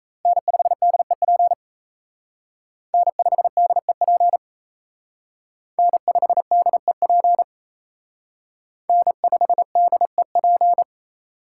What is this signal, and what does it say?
Morse code, heard as a keyed tone, about 32 wpm, 700 Hz. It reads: N5DEP N5DEP N5DEP N5DEP
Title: Speed Racing - USA 1x3 Call Signs 28wpm